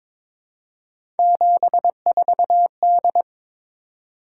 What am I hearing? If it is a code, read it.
Morse code, heard as a keyed tone, about 22 wpm, 700 Hz. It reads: 74D
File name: practice / 22wpm / Three Letter-Number Combinations - 22wpm